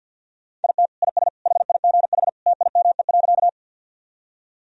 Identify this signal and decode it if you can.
Morse code, heard as a keyed tone, about 47 wpm, 700 Hz. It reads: IT IS HIGH TIME.
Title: Sentences from Top 1000 Words - Review Part 1 - 50wpm